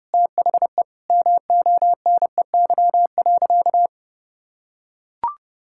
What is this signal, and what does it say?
Morse code, heard as a keyed tone, about 30 wpm, 700 Hz. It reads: THE MONEY.
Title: Sentences from Top 1000 Words - Review Part 6 - 30wpm